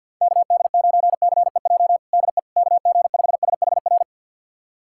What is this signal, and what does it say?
Morse code, heard as a keyed tone, about 53 wpm, 690 Hz. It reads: KD9CEJ DE KG5SHR
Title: USA Callsign DE Callsign - 6 Characters - 50wpm